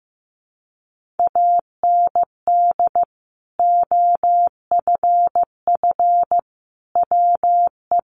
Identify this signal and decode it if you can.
Morse code, heard as a keyed tone, about 15 wpm, 700 Hz. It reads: AND OFF WE
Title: Sentences from Top 200 Words 15wpm